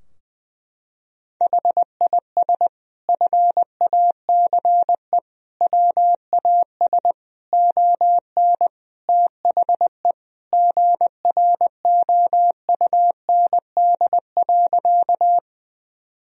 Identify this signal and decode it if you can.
Morse code, heard as a keyed tone, about 20 wpm, 700 Hz. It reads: HIS FACE WAS ON THE GROUND.
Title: Sentences from Top 1000 Words - Review Part 2 - 20wpm